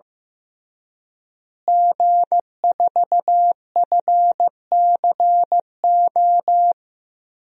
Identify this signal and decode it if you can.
Morse code, heard as a keyed tone, about 15 wpm, 700 Hz. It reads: G4FCO
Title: DXCC-100 Call Signs - 5 Characters - Repeated 3x - 15wpm